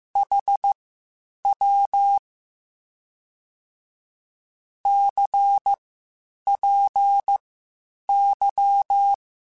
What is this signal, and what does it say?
Morse code, heard as a keyed tone, about 15 wpm, spaced as for 8 wpm, 800 Hz. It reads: HW CPY
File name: (spaced) / QSOs Spaced-mp3s / WD0BGP 1 @13 sp